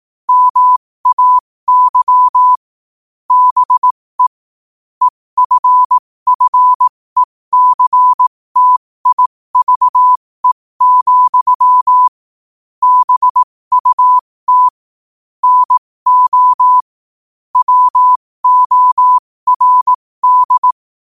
Morse code, heard as a keyed tone, about 18 wpm, spaced as for 15 wpm, 1000 Hz. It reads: MAY BE EFFECTIVE, BUT NO WORD
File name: qotdfiles / 15wpm